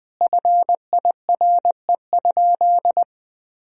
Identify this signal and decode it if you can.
Morse code, heard as a keyed tone, about 20 wpm, 700 Hz. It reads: FIRE?
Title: Sentences from Top 1000 Words - Review Part 3 - 20wpm